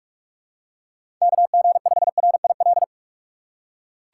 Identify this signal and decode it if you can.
Morse code, heard as a keyed tone, about 45 wpm, 700 Hz. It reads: KG5RIL
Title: USA Call Signs - 2x Character Spacing - 45wpm